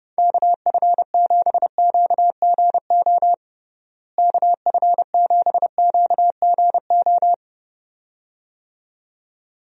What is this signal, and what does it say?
Morse code, heard as a keyed tone, about 32 wpm, 700 Hz. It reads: KF7QGO KF7QGO
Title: USA Call Signs - 6 Characters - Repeated 2x - 30wpm